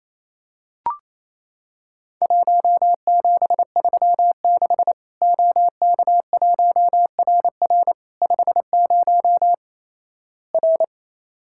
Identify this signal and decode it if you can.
Morse code, heard as a keyed tone, about 28 wpm, 700 Hz. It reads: 1736 OK1RR 50 R
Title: Worked All Europe DX Contest - QTC Exchange - 28wpm